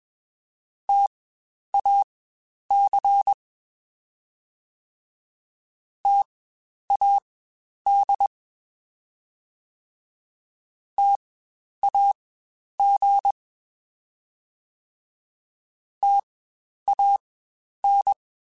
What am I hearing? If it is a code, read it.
Morse code, heard as a keyed tone, about 21 wpm, spaced as for 10 wpm, 800 Hz. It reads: TAC TAD TAG TAN
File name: Spaced Alphas / Spaced Alphas mp3s / Ts sp @20